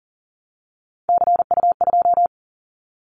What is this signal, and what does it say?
Morse code, heard as a keyed tone, about 38 wpm, 700 Hz. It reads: CU2
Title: Callsign Prefixes for Top 100 DXCC - 40wpm